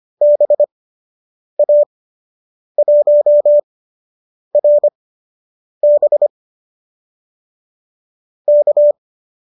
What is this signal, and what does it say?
Morse code, heard as a keyed tone, about 25 wpm, spaced as for 8 wpm, 570 Hz. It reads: BA1RB K